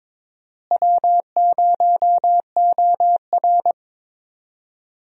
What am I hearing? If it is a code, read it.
Morse code, heard as a keyed tone, about 22 wpm, 700 Hz. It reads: W0OR
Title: USA Call Signs - 4 Characters - Repeated 3x - 22wpm